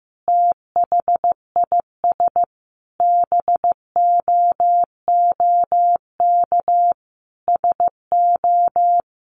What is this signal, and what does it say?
Morse code, heard as a keyed tone, about 15 wpm, 700 Hz. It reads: THIS BOOK SO
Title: Sentences from Top 1200 Words 15wpm